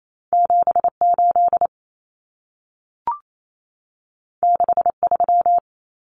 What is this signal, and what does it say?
Morse code, heard as a keyed tone, about 28 wpm, 700 Hz. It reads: 78 63
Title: Numbers - 2-Digit 28wpm